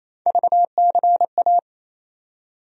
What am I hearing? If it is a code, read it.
Morse code, heard as a keyed tone, about 27 wpm, 700 Hz. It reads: VCA
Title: USA Call Sign Postfixes - 3 Letters 28wpm